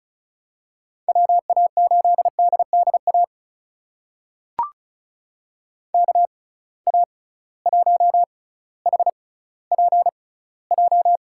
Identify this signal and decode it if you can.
Morse code, heard as a keyed tone, about 36 wpm, spaced as for 18 wpm, 700 Hz. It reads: WA8DDA KA1HPJ
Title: USA Call Signs - 6 Characters - 6x Character Spacing - 35wpm